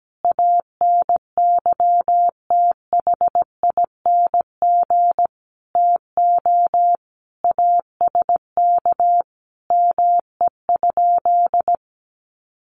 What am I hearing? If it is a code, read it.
Morse code, heard as a keyed tone, about 17 wpm, 700 Hz. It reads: ANYTHING TO ASK ME?
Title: Sentences from Top 1000 Words 17wpm